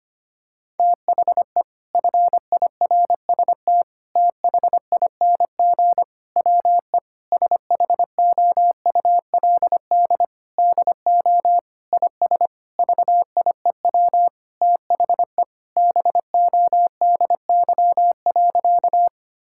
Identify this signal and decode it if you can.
Morse code, heard as a keyed tone, about 25 wpm, 700 Hz. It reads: THE FIRST THING WE SHOULD DO IS VIEW THE BODY.
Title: Sentences from Top 1000 Words - Review Part 6 - 25wpm